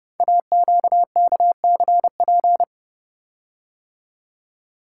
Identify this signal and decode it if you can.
Morse code, heard as a keyed tone, about 30 wpm, 700 Hz. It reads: AQKCP
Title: Code Groups - 5 Letters - 30wpm